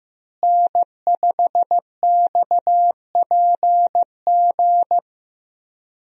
Morse code, heard as a keyed tone, about 15 wpm, 700 Hz. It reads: N5XPG
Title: USA Call Signs - 5 Characters - Repeated 3x - 15wpm